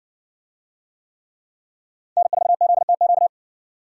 Morse code, heard as a keyed tone, about 60 wpm, 700 Hz. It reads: N4ZTQ